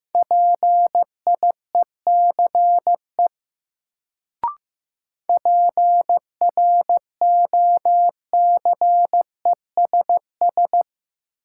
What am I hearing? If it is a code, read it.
Morse code, heard as a keyed tone, about 15 wpm, 700 Hz. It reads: PIECE PROCESS
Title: Sets of 3 Words - New Word Drill - 15wpm